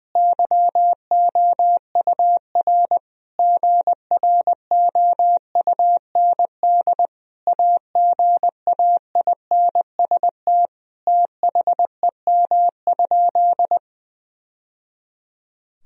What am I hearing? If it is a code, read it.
Morse code, heard as a keyed tone, about 20 wpm, 700 Hz. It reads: YOUR GROUND AGAINST THEM?
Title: Sentences from Top 1000 Words - Review Part 2 - 20wpm